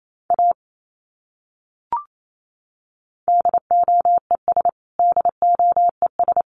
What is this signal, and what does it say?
Morse code, heard as a keyed tone, about 28 wpm, 700 Hz. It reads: A DOES DOES
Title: Top 200 Words Repeated 28wpm